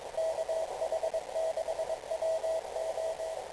Morse code, heard as a keyed tone, about 22 wpm, 650 Hz, with radio noise. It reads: KH6WO
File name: kl7fdq / beacons / kh6wo